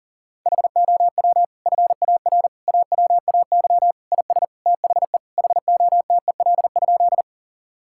Morse code, heard as a keyed tone, about 40 wpm, 700 Hz. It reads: HOW FAR AWAY IS THE HOTEL?
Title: Sentences from Top 1600 Words 40wpm